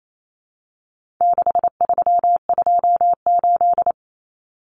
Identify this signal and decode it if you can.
Morse code, heard as a keyed tone, about 28 wpm, 700 Hz. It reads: 6328
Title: Common Cut Numbers - 4-Digit 28wpm